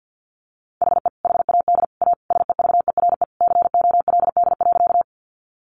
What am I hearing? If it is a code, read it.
Morse code, heard as a keyed tone, about 50 wpm, 700 Hz. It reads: HE HAD A SEVERE COLD.
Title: Sentences from Top 1400 Words 50wpm